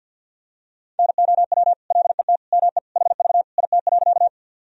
Morse code, heard as a keyed tone, about 52 wpm, 700 Hz. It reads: NOW LET ME HV IT.
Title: Sentences from Top 300 Words and Abbreviations 50wpm